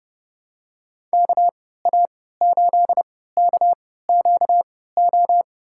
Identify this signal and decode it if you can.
Morse code, heard as a keyed tone, about 30 wpm, spaced as for 17 wpm, 700 Hz. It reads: KA8KQO
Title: USA Call Signs - 6 Characters - 3x Character Spacing - 30wpm